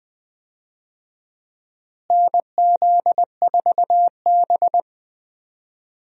Morse code, heard as a keyed tone, about 20 wpm, 700 Hz. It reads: NZ4B
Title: USA Call Signs - 4 Characters - Repeated 3x - 20wpm